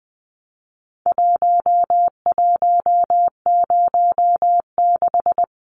Morse code, heard as a keyed tone, about 20 wpm, 700 Hz. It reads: 1106
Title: Numbers - Contest Style - 4-Digit 20wpm